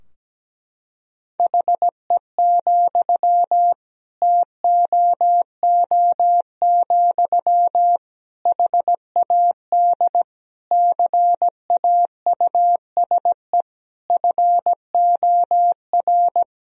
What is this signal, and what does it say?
Morse code, heard as a keyed tone, about 17 wpm, 700 Hz. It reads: HE, TOO, HAD CAUSE FOR
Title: Sentences from Top 1000 Words - Review Part 3 - 17wpm